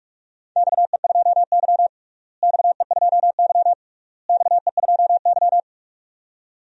Morse code, heard as a keyed tone, about 46 wpm, 700 Hz. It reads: XE2Y XE2Y XE2Y
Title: DXCC-100 Call Signs - 4 Characters - Repeated 3x - 45wpm